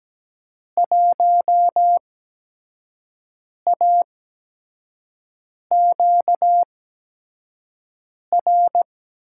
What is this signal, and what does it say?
Morse code, heard as a keyed tone, about 17 wpm, spaced as for 5 wpm, 700 Hz. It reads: 1AQR
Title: USA Call Signs - 8x Character Spacing - 17wpm